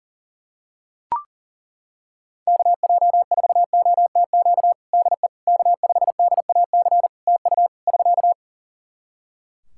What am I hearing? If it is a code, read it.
Morse code, heard as a keyed tone, about 40 wpm, 700 Hz. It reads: KJ4OTQ DE K5DAC TU <SK>
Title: Practice QSO - Exchange 4 - Ending - 40wpm